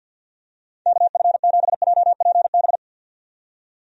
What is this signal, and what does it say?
Morse code, heard as a keyed tone, about 51 wpm, 700 Hz. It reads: KF7JPB
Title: USA Call Signs - 6 Characters - 50wpm